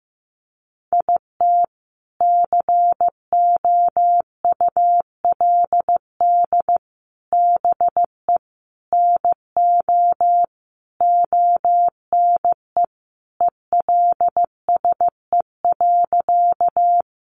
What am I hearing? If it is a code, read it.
Morse code, heard as a keyed tone, about 15 wpm, 700 Hz. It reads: IT COULD BE NO ONE ELSE.
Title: Sentences from Top 800 Words 15wpm